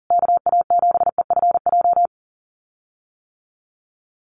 Morse code, heard as a keyed tone, about 40 wpm, 700 Hz. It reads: KA7EFJ
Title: USA Call Signs Repeated 2x 40wpm